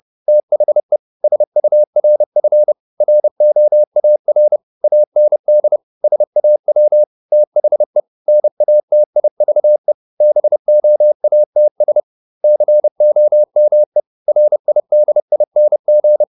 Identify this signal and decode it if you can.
Morse code, heard as a keyed tone, about 30 wpm, 600 Hz. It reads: THE SURF ROAR AND SAW THE NATIVE BOATS COME RIDING
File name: the old man and the sea / the old man and the sea 1 30wpm